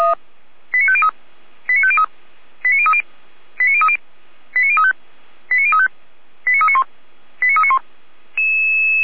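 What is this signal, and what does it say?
Five-tone selective call (ZVEI), heard as quick runs of five tones, tones of about 70 ms. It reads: 89503 89503 89030 89030 89035 89035 89391 89391
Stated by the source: ZVEI five-tone selective call 89092 received by SDR.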